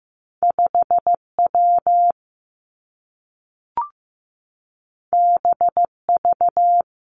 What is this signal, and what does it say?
Morse code, heard as a keyed tone, about 15 wpm, 700 Hz. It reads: 5 W B V
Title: Two Challenging Letter-Number Combinations 15wpm